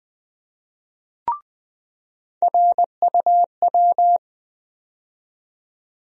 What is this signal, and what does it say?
Morse code, heard as a keyed tone, about 21 wpm, 700 Hz. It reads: RUW
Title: USA Call Sign Postfixes - 3 Letters 20wpm